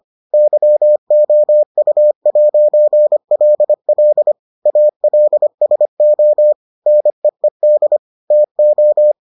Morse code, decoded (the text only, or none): YOU'LL ALSO NEED TO